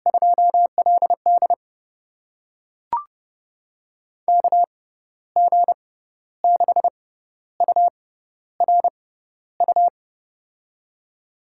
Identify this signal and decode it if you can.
Morse code, heard as a keyed tone, about 30 wpm, spaced as for 13 wpm, 700 Hz. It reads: 2LD KG6URU
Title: USA Call Signs - 6x Character Spacing - 30wpm